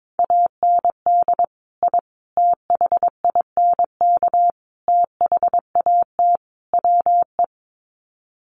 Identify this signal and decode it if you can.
Morse code, heard as a keyed tone, about 22 wpm, 700 Hz. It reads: AND I THINK THAT WE
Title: Sets of 5 Words Encore2 22wpm